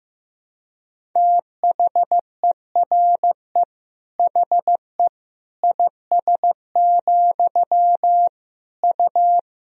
Morse code, heard as a keyed tone, about 15 wpm, 700 Hz. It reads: THERE HE IS, U
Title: Sentences from Top 100 Words and Abbreviations 15wpm